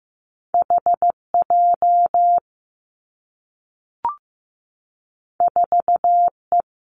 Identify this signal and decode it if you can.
Morse code, heard as a keyed tone, about 15 wpm, 700 Hz. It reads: H J 4 E
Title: Two Challenging Letter-Number Combinations 15wpm